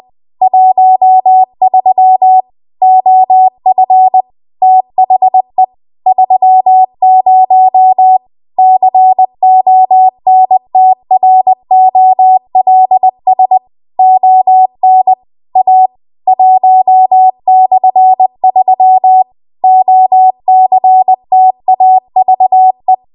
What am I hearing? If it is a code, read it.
Morse code, heard as a keyed tone, about 20 wpm, 750 Hz. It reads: 13 OF THE 30 CONTROLS ON A 1/3 OCTAVE